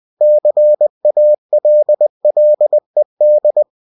Morse code, heard as a keyed tone, about 20 wpm, 600 Hz. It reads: CALLED
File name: cw-books / winnie-the-pooh / winnie-20wpm-page-0080